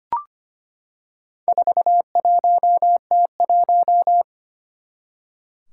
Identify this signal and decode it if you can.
Morse code, heard as a keyed tone, about 25 wpm, 700 Hz. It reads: 41T1
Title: Four Challenging Letter-Number Combinations 25wpm